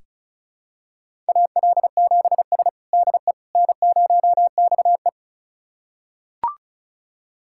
Morse code, heard as a keyed tone, about 36 wpm, 700 Hz. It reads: AL7S DE N0XE
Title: USA Callsign DE Callsign - 4 Characters - 35wpm